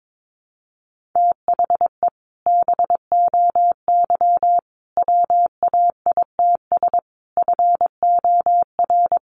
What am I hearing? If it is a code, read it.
Morse code, heard as a keyed tone, about 22 wpm, 700 Hz. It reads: THE BOY WAITS FOR
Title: Additional Easy Phrases 22wpm